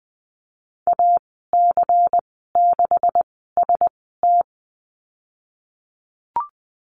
Morse code, heard as a keyed tone, about 20 wpm, spaced as for 15 wpm, 700 Hz. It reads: AC6ST
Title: 179 Lesson 29g - Slash - Call Signs 20wpm